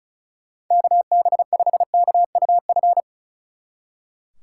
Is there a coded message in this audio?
Yes: Morse code from 0.7 to 3.0 s, heard as a keyed tone, about 45 wpm, 700 Hz. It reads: KB5KUF